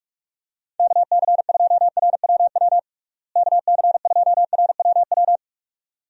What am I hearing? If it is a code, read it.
Morse code, heard as a keyed tone, about 46 wpm, 700 Hz. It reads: KC2RWW KC2RWW
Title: USA Call Signs - 6 Characters - Repeated 2x - 45wpm